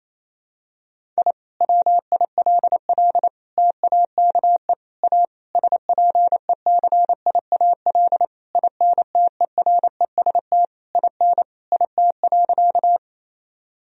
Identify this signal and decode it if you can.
Morse code, heard as a keyed tone, about 28 wpm, 700 Hz. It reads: I WILL TAKE A SPECIAL INTEREST IN IT.
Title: Sentences from Top 1000 Words - Review Part 3 - 28wpm